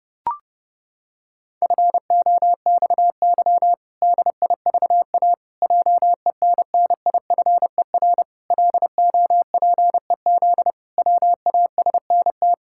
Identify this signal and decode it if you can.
Morse code, heard as a keyed tone, about 30 wpm, 700 Hz. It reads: FOXY DIVA JENNIFER LOPEZ WASNT
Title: Pangram Challenge v2 - 30wpm